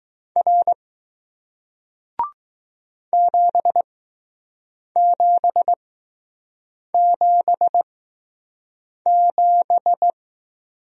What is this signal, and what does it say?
Morse code, heard as a keyed tone, about 20 wpm, spaced as for 6 wpm, 700 Hz. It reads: R 7777